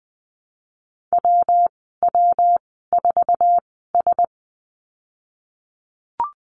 Morse code, heard as a keyed tone, about 20 wpm, spaced as for 14 wpm, 700 Hz. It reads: WW4S
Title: 173 Lesson 28g - Number 9 - Call Signs 20wpm